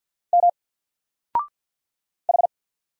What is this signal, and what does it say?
Morse code, heard as a keyed tone, about 50 wpm, 700 Hz. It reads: M H
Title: Single Character (Letter:Number:Punctuation:Prosign) 50wpm